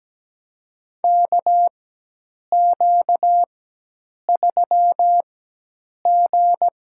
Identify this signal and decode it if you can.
Morse code, heard as a keyed tone, about 17 wpm, spaced as for 8 wpm, 700 Hz. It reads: KQ3G
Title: USA Call Signs - 4 Characters - 4x Character Spacing - 17wpm